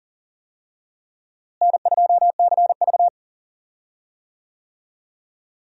Morse code, heard as a keyed tone, about 40 wpm, 700 Hz. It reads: N2CV
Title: USA Call Signs - 4 Characters - Repeated 2x - 40wpm